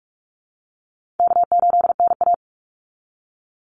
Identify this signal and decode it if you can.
Morse code, heard as a keyed tone, about 44 wpm, 700 Hz. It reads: K8NA
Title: Easy Call Signs Repeated 3x - 45wpm